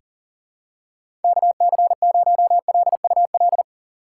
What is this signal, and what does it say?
Morse code, heard as a keyed tone, about 40 wpm, 700 Hz. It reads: KC0LUL
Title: USA Call Signs - 6 Characters - 4x Character Spacing - 40wpm